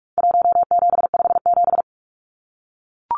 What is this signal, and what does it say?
Morse code, heard as a keyed tone, about 44 wpm, 700 Hz. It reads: 1757
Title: Numbers - Contest Style - 4-Digit 45wpm